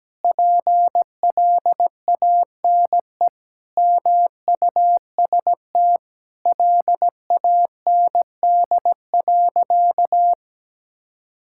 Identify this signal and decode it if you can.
Morse code, heard as a keyed tone, about 17 wpm, 700 Hz. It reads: PLANE MUST LAND.
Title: Sentences from Top 1000 Words - Review Part 6 - 17wpm